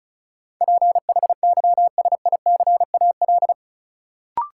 Morse code, heard as a keyed tone, about 36 wpm, 700 Hz. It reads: PHYSICAL
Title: Sentences from Top 1400 Words - New Word Drill Alt - 35wpm